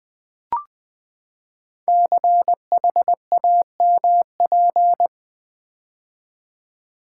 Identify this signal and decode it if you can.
Morse code, heard as a keyed tone, about 20 wpm, 700 Hz. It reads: CHAMP
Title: Build A Word - 5 Letters At A Time - 12-Letter Words - 20wpm